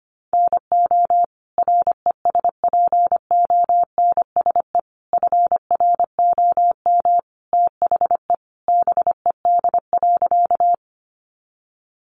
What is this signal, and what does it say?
Morse code, heard as a keyed tone, about 25 wpm, 700 Hz. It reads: NO RESPONSE FROM THE BED.
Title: Sentences from Top 1400 Words 25wpm